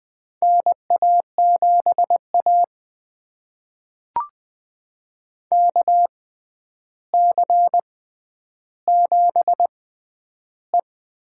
Morse code, heard as a keyed tone, about 20 wpm, spaced as for 10 wpm, 700 Hz. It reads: NA7A KC7E
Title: USA Call Signs - 4 Characters - 6x Character Spacing - 20wpm